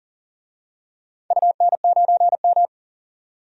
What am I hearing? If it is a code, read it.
Morse code, heard as a keyed tone, about 39 wpm, 700 Hz. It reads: UN9M
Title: DXCC-100 Call Signs - 4 Characters - 40wpm